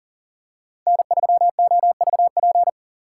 Morse code, heard as a keyed tone, about 40 wpm, 700 Hz. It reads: N3OVP
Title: USA Call Signs - 6x Character Spacing - 40wpm